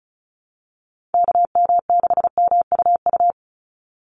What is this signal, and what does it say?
Morse code, heard as a keyed tone, about 34 wpm, 700 Hz. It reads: KM6MUU
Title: USA Call Signs 35wpm